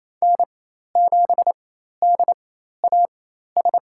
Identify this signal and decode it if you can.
Morse code, heard as a keyed tone, about 27 wpm, spaced as for 13 wpm, 700 Hz. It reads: N7DAS
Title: USA Call Signs - 4x Character Spacing - 28wpm